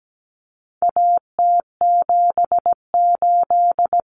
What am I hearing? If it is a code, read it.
Morse code, heard as a keyed tone, about 17 wpm, 700 Hz. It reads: AT78
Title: Common Cut Numbers - 4-Digit 17wpm